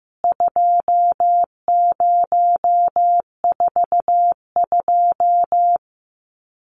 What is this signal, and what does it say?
Morse code, heard as a keyed tone, about 15 wpm, 700 Hz. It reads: 2042
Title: CWT Exchange 15wpm